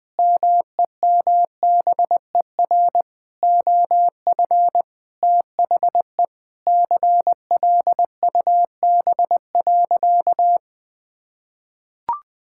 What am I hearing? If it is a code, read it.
Morse code, heard as a keyed tone, about 20 wpm, 700 Hz. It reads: MEMBER OF THE CLUB.
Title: Sentences with 5-Word Phrases - 20wpm